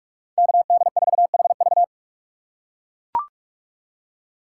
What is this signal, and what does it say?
Morse code, heard as a keyed tone, about 44 wpm, 700 Hz. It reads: KD4HV
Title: USA Call Signs - 5 Characters - 2x Character Spacing - 45wpm